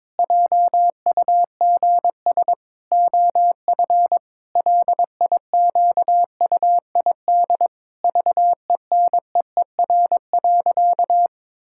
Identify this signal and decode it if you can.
Morse code, heard as a keyed tone, about 21 wpm, 700 Hz. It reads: JUGS OF LIQUID VENEER.
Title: Pangram Challenge v2 - 22wpm